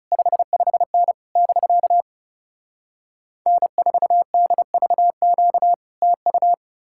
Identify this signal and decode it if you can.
Morse code, heard as a keyed tone, about 40 wpm, 700 Hz. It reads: 55N <BK> N4DVQ TU